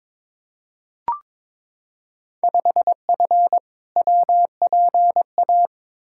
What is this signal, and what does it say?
Morse code, heard as a keyed tone, about 22 wpm, 700 Hz. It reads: 5F WPA
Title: Field Day Exchanges Encore - 22wpm